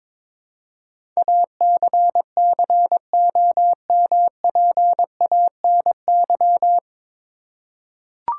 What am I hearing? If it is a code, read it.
Morse code, heard as a keyed tone, about 22 wpm, 700 Hz. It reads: ACCOMPANY
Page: Morse Code Ninja practice 22wpm